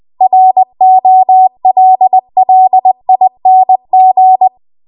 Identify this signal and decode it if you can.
Morse code, heard as a keyed tone, about 20 wpm, 750 Hz. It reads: ROLLING
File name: Morse / 20WPM